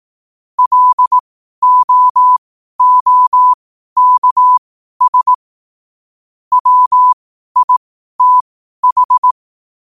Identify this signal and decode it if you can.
Morse code, heard as a keyed tone, about 18 wpm, spaced as for 12 wpm, 1000 Hz. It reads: LOOKS WITH
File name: qotdfiles / 12wpm